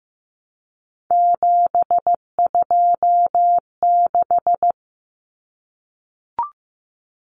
Morse code, heard as a keyed tone, about 15 wpm, 700 Hz. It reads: 726
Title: Numbers - 3-Digit 15wpm